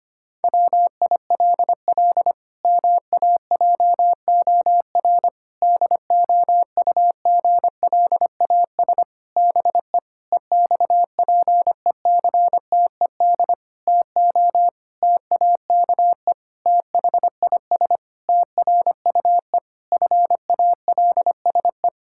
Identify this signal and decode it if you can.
Morse code, heard as a keyed tone, about 25 wpm, 700 Hz. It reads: WILL MAJOR DOUGLAS BE EXPECTED TO TAKE THIS TRUE FALSE
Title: Pangram Challenge v2 - 25wpm